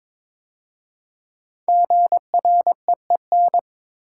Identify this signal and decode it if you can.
Morse code, heard as a keyed tone, about 23 wpm, 700 Hz. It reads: GREEN
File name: 5 Letter Words - First 3 Letters Given - 22wpm